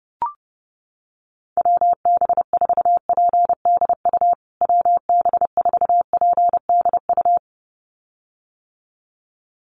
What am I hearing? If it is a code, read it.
Morse code, heard as a keyed tone, about 30 wpm, 700 Hz. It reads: WB4PDU WB4PDU
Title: USA Call Signs Repeated 2x 30wpm